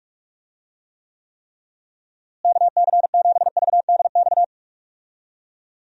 Morse code, heard as a keyed tone, about 43 wpm, 700 Hz. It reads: KC7VDX
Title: USA Call Signs - 6 Characters - Repeated 3x - 45wpm